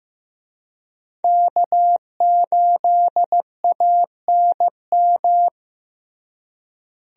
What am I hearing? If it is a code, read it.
Morse code, heard as a keyed tone, about 15 wpm, 700 Hz. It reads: K8ANM
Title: USA Call Signs - 5 Characters - Repeated 3x - 15wpm